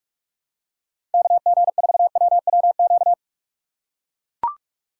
Morse code, heard as a keyed tone, about 43 wpm, 700 Hz. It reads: KG4WWQ
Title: USA Call Signs - 6 Characters - 6x Character Spacing - 45wpm